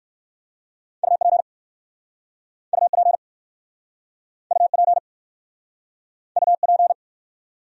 Morse code, heard as a keyed tone, about 59 wpm, 700 Hz. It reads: UP UP UP UP